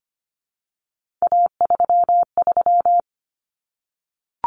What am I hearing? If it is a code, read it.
Morse code, heard as a keyed tone, about 25 wpm, 700 Hz. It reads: A33
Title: Common Cut Numbers - 3-Digit 25wpm